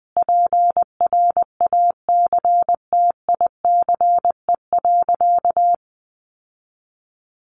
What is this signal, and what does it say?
Morse code, heard as a keyed tone, about 20 wpm, 700 Hz. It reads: PRACTICE.
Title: Sentences from Top 900 Words 20wpm